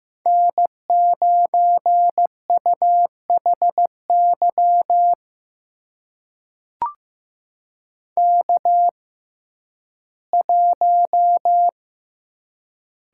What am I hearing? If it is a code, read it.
Morse code, heard as a keyed tone, about 15 wpm, spaced as for 11 wpm, 700 Hz. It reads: N9UHY K1
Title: USA Call Signs - 5 Characters - 6x Character Spacing - 15wpm